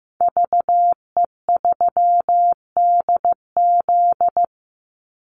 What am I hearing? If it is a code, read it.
Morse code, heard as a keyed tone, about 15 wpm, 700 Hz. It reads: VE3DZ
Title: Call Signs Repeated 3x - 15wpm